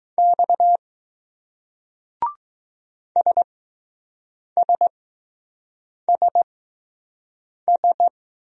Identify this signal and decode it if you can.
Morse code, heard as a keyed tone, about 20 wpm, spaced as for 6 wpm, 700 Hz. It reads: X SSSS